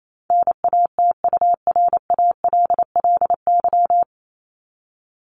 Morse code, heard as a keyed tone, about 28 wpm, 700 Hz. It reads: NATURALLY
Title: LY Suffix - Top 100 Words with Base Words 28wpm